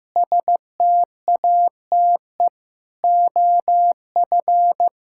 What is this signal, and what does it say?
Morse code, heard as a keyed tone, about 15 wpm, 700 Hz. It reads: STATE OF
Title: Sentences from Top 1000 Words - Review Part 6 - 15wpm